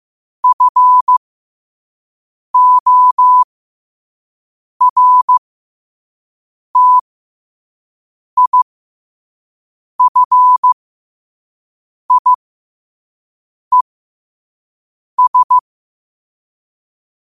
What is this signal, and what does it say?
Morse code, heard as a keyed tone, about 15 wpm, spaced as for 5 wpm, 1000 Hz. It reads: FORTIFIES
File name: qotdfiles / 5wpm